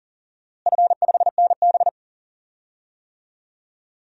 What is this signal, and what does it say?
Morse code, heard as a keyed tone, about 40 wpm, 700 Hz. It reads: F5NB
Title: DXCC-100 Call Signs - 4 Characters - Repeated 2x - 40wpm